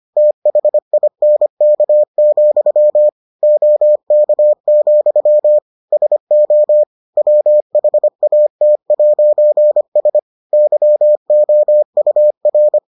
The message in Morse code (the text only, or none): THINK, OK, SO WHAT'S YOUR